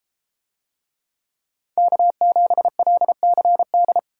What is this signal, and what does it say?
Morse code, heard as a keyed tone, about 30 wpm, 700 Hz. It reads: K7LCD